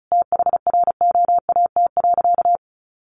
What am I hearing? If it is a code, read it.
Morse code, heard as a keyed tone, about 34 wpm, 700 Hz. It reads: THROAT.
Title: A Sound Of Thunder 35wpm